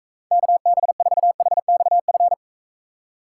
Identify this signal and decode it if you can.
Morse code, heard as a keyed tone, about 43 wpm, 700 Hz. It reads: KB4HXF